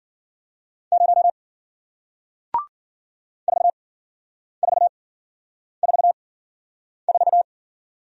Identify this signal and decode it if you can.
Morse code, heard as a keyed tone, about 52 wpm, spaced as for 9 wpm, 700 Hz. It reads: 0 4444